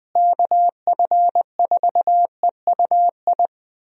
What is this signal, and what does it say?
Morse code, heard as a keyed tone, about 20 wpm, 700 Hz. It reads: KF4EUI
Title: USA Call Signs - 6 Characters - Repeated 2x - 20wpm